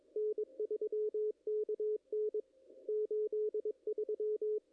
Morse code, heard as a keyed tone, about 20 wpm, 440 Hz, with radio noise. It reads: N3KN 83
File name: contests / ss-cw-k5tr-op-n5rz -0100